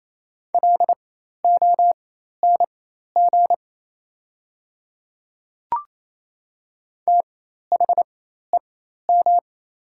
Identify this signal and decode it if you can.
Morse code, heard as a keyed tone, about 28 wpm, spaced as for 13 wpm, 700 Hz. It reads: LONG THEM
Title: 4x Character Spacing - Top 100 Words - 28wpm